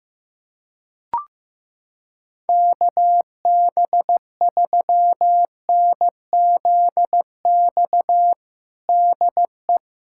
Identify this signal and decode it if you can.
Morse code, heard as a keyed tone, about 15 wpm, 700 Hz. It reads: KB3NZX DE
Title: USA Callsign DE Callsign 15wpm